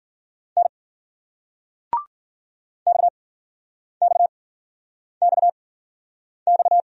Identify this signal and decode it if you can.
Morse code, heard as a keyed tone, about 56 wpm, spaced as for 9 wpm, 700 Hz. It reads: N XXXX